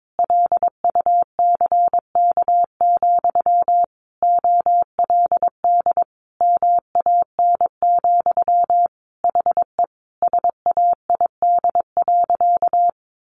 Morse code, heard as a keyed tone, about 22 wpm, 700 Hz. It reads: LUCK, OLD MAN, HE SAID.
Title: Sentences from Top 1500 Words 22wpm